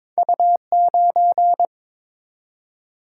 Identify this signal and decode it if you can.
Morse code, heard as a keyed tone, about 23 wpm, 700 Hz. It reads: U9
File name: practice / 22wpm / Two Letter-Number Combinations - 22wpm